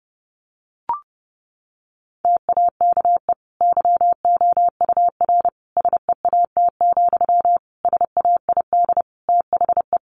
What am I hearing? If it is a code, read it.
Morse code, heard as a keyed tone, about 30 wpm, 700 Hz. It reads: TAKE YOUR SEAT, SAID THE
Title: Sentences from Top 1500 Words 30wpm